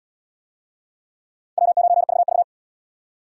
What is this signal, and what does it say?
Morse code, heard as a keyed tone, about 69 wpm, 700 Hz. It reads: W9VV